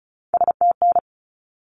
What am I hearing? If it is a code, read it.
Morse code, heard as a keyed tone, about 34 wpm, 700 Hz. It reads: STN
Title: Abbreviations 35wpm